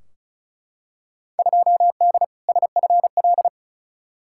Morse code, heard as a keyed tone, about 36 wpm, 700 Hz. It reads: 2D SFL
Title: Field Day Exchanges Encore - 35wpm